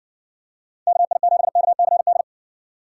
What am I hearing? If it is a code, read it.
Morse code, heard as a keyed tone, about 60 wpm, 700 Hz. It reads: KI7KCD